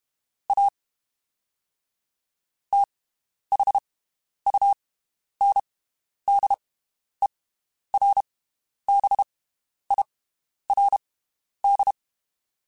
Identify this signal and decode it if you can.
Morse code, heard as a keyed tone, about 32 wpm, spaced as for 11 wpm, 800 Hz. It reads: A THUNDERBIRD
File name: (spaced) / QSOs Spaced-mp3s / WD0BGP 1 @30 sp